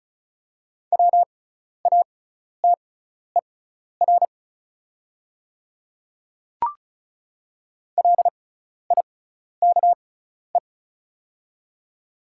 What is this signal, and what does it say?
Morse code, heard as a keyed tone, about 33 wpm, spaced as for 12 wpm, 690 Hz. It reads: WATER LIKE
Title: 6x Character Spacing - Top 100 Words - 35wpm